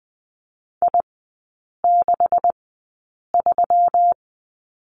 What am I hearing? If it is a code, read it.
Morse code, heard as a keyed tone, about 20 wpm, spaced as for 8 wpm, 700 Hz. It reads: I63
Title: 184 Lesson 30e - Letter b - Three Characters 20wpm